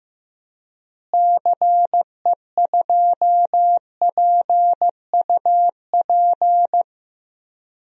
Morse code, heard as a keyed tone, about 15 wpm, 700 Hz. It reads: CE2PUP
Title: DXCC-100 Call Signs - 6 Characters - Repeated 3x - 15wpm